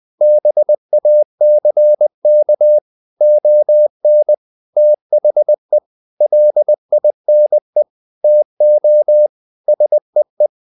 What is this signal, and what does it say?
Morse code, heard as a keyed tone, about 20 wpm, 600 Hz. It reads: BACK ON THE LINE TO SEE
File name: the old man and the sea part 2 20wpm